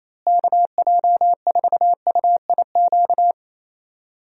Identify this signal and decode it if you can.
Morse code, heard as a keyed tone, about 28 wpm, 700 Hz. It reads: KJ4UIQ
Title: USA Call Signs - 6 Characters - Repeated 2x - 28wpm